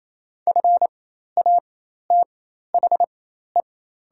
Morse code, heard as a keyed tone, about 27 wpm, spaced as for 13 wpm, 700 Hz. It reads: FATHE
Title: 4x Character Spacing - Top 500 Words - 28wpm